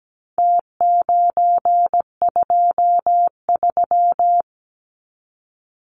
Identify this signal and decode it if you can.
Morse code, heard as a keyed tone, about 17 wpm, 700 Hz. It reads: T923
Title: Numbers - Contest Style - 4-Digit 17wpm